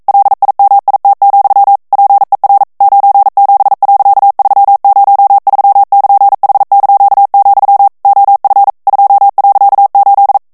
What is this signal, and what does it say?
Morse code, heard as a keyed tone, about 42 wpm, 800 Hz. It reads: LIMIT, PER 97.303<KN>H), OF 2.8